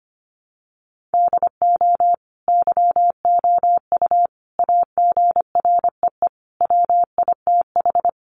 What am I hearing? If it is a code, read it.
Morse code, heard as a keyed tone, about 25 wpm, 700 Hz. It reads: DO YOU AGREE WITH
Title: Sentences from Top 1400 Words 25wpm